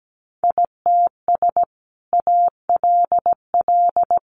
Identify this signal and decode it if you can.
Morse code, heard as a keyed tone, about 17 wpm, 700 Hz. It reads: ITS ALL
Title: Sentences from Top 600 Words 17wpm